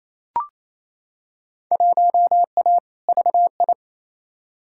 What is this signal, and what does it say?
Morse code, heard as a keyed tone, about 28 wpm, 700 Hz. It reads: 1A VI
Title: Field Day Exchanges Encore - 28wpm